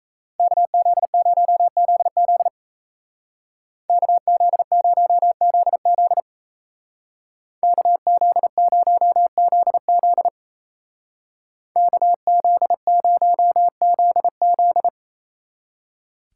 Morse code, heard as a keyed tone, about 34 wpm, 700 Hz. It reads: KZ0ZZ KZ0ZZ KZ0ZZ KZ0ZZ
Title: Speed Racing - USA 2x2 Call Signs 28wpm